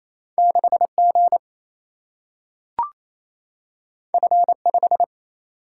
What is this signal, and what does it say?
Morse code, heard as a keyed tone, about 28 wpm, 700 Hz. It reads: 6 G F 5
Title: Single-Letter Single-Number 28wpm